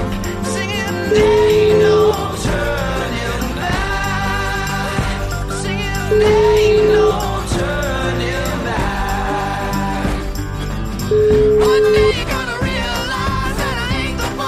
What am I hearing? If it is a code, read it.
Ringback tone (Europe, 425 Hz), with noise.